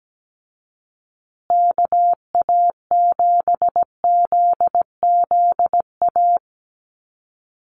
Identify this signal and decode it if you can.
Morse code, heard as a keyed tone, about 17 wpm, 700 Hz. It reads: KA7ZZA
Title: USA Call Signs Repeated 3x 17wpm